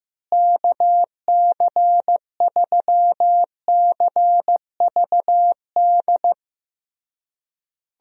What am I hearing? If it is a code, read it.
Morse code, heard as a keyed tone, about 16 wpm, 700 Hz. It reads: KC3CVD
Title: USA Call Signs - 6 Characters - Repeated 3x - 15wpm